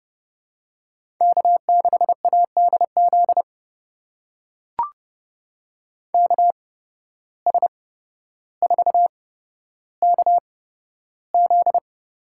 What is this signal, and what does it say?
Morse code, heard as a keyed tone, about 30 wpm, spaced as for 13 wpm, 700 Hz. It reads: K6ADZ KS4KZ
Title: USA Call Signs - 8x Character Spacing - 30wpm